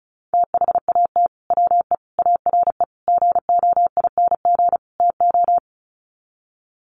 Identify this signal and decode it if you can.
Morse code, heard as a keyed tone, about 34 wpm, 700 Hz. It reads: THAT WE ARE GOING TO
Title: Sets of 5 Words Encore 35wpm